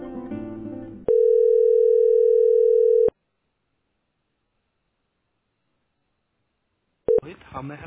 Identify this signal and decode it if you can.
Ringback tone (North America), 440 + 480 Hz.